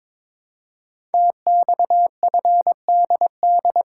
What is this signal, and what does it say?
Morse code, heard as a keyed tone, about 22 wpm, 700 Hz. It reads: TXFDD
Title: Code Groups - 5 Letters - 22wpm